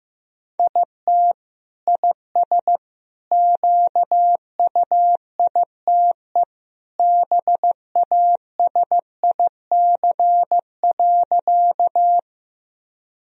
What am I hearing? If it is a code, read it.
Morse code, heard as a keyed tone, about 15 wpm, 700 Hz. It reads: IT IS QUITE BASIC.
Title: Sentences from Top 1000 Words - Review Part 3 - 15wpm